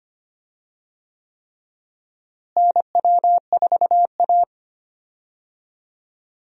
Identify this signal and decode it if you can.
Morse code, heard as a keyed tone, about 25 wpm, 700 Hz. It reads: NW4A
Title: USA Call Signs - 4 Characters - Repeated 3x - 25wpm